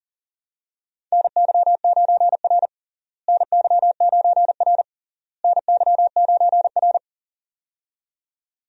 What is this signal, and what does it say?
Morse code, heard as a keyed tone, about 40 wpm, 700 Hz. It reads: NY9R NY9R NY9R
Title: USA Call Signs - 4 Characters - Repeated 3x - 40wpm